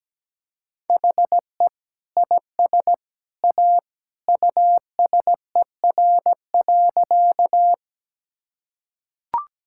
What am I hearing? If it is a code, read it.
Morse code, heard as a keyed tone, about 17 wpm, 700 Hz. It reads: HE IS A USER.
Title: Sentences from Top 1000 Words - Review Part 5 - 17wpm